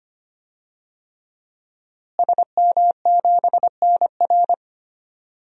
Morse code, heard as a keyed tone, about 25 wpm, 700 Hz. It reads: SM7NR
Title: DXCC-100 Call Signs - 5 Characters - Repeated 3x - 25wpm